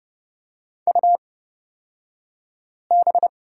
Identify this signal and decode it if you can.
Morse code, heard as a keyed tone, about 30 wpm, spaced as for 5 wpm, 700 Hz. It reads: UB
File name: Lesson 36d - BT - Two Characters 30wpm